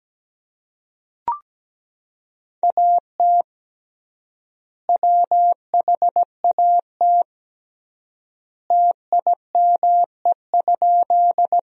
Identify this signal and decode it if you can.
Morse code, heard as a keyed tone, about 17 wpm, 700 Hz. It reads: AT WHAT TIME?
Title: 3x Word-Spacing - Sentences from Top 100 Words - 17wpm